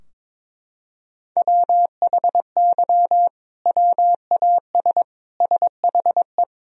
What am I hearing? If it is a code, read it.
Morse code, heard as a keyed tone, about 21 wpm, 700 Hz. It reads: WHY WAS SHE
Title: Sentences from Top 1000 Words - Review Part 3 - 22wpm